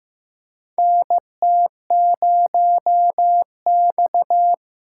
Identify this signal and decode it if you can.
Morse code, heard as a keyed tone, about 15 wpm, 700 Hz. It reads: NT0X
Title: USA Call Signs - 4 Characters - 6x Character Spacing - 15wpm